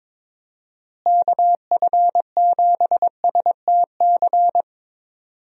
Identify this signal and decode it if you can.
Morse code, heard as a keyed tone, about 22 wpm, 700 Hz. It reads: KF7STC
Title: USA Call Signs - 6 Characters - Repeated 3x - 22wpm